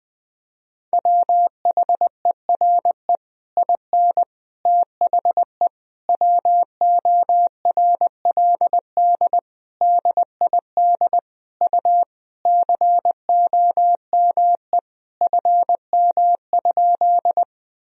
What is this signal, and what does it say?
Morse code, heard as a keyed tone, about 20 wpm, 700 Hz. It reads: WHERE IN THE WORLD DID U COME FM?
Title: Sentences from Top 200 Words and Abbreviations 20wpm